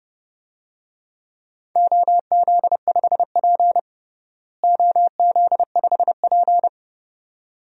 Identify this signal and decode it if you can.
Morse code, heard as a keyed tone, about 29 wpm, 700 Hz. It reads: OZ5P OZ5P
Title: DXCC-100 Call Signs - 4 Characters - Repeated 2x - 30wpm